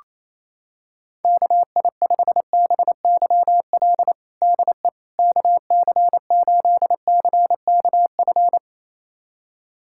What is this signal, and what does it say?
Morse code, heard as a keyed tone, about 28 wpm, 700 Hz. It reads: KI5BYL DE KC8CKF
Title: USA Callsign DE Callsign - 6 Characters - 28wpm